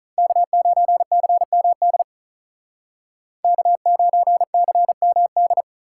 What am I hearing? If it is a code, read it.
Morse code, heard as a keyed tone, about 44 wpm, 700 Hz. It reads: K9CMD K9CMD